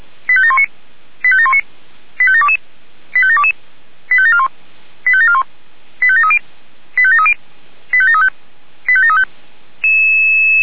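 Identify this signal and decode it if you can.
Five-tone selective call (ZVEI), heard as quick runs of five tones, tones of about 70 ms. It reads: 85719 85719 85722 85722 85741 85741 85730 85730 85735 85735
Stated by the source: Five-tone ZVEI selective call received by SDR.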